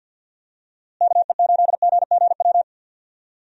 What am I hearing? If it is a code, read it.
Morse code, heard as a keyed tone, about 46 wpm, 700 Hz. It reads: KE8GGW
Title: USA Call Signs - 6 Characters - Repeated 3x - 50wpm